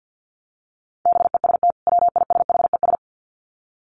Morse code, heard as a keyed tone, about 50 wpm, 700 Hz. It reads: BEST WISHES
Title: Common QSO Phrases 50wpm